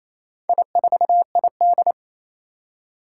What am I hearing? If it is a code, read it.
Morse code, heard as a keyed tone, about 27 wpm, 700 Hz. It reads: I4ID
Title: Four Challenging Letter-Number Combinations 28wpm